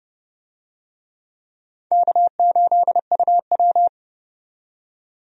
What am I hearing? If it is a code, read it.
Morse code, heard as a keyed tone, about 29 wpm, 700 Hz. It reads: K8UW
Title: USA Call Signs - 4 Characters - 30wpm